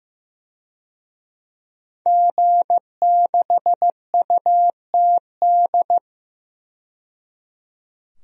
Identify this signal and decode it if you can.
Morse code, heard as a keyed tone, about 15 wpm, 700 Hz. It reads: G6UTD
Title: DXCC-100 Call Signs - 5 Characters - 15wpm